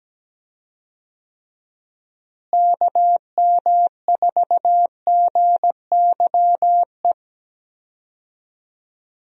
Morse code, heard as a keyed tone, about 17 wpm, 700 Hz. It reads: KM4GYE
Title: USA Call Signs - 6 Characters - Repeated 3x - 17wpm